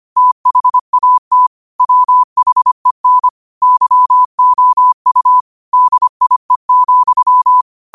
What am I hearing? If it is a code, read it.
Morse code, heard as a keyed tone, about 25 wpm, 1000 Hz. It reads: THAT WHEN YOU DIE,
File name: qotdfiles / 25wpm